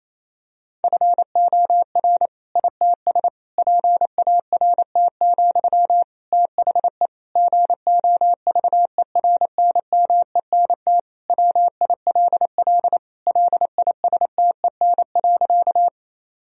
Morse code, heard as a keyed tone, about 28 wpm, 700 Hz. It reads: FOR ITS PART, THE GOVERNMENT WILL LISTEN.
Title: Sentences from Top 1000 Words - Review Part 3 - 28wpm